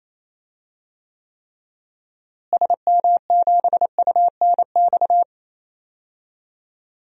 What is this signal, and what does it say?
Morse code, heard as a keyed tone, about 27 wpm, 700 Hz. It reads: SM7UNX
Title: DXCC-100 Call Signs - 6 Characters - Repeated 3x - 28wpm